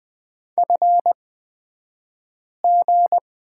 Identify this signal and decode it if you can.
Morse code, heard as a keyed tone, about 20 wpm, spaced as for 5 wpm, 700 Hz. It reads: FG
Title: 242 Lesson 39d - Letter z - Two Characters 20wpm